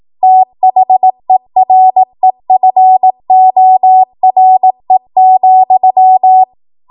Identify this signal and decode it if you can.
Morse code, heard as a keyed tone, about 18 wpm, 750 Hz. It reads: THEREFORE,
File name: Morse / 18WPM